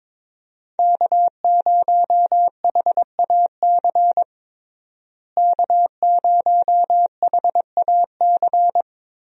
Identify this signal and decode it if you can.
Morse code, heard as a keyed tone, about 22 wpm, 700 Hz. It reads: K0HAC K0HAC
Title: USA Call Signs - 5 Characters - Repeated 3x - 22wpm